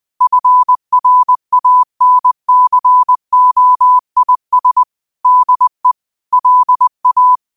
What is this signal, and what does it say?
Morse code, heard as a keyed tone, about 20 wpm, 1000 Hz. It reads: FRANCOIS DE LA
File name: qotdfiles / 20wpm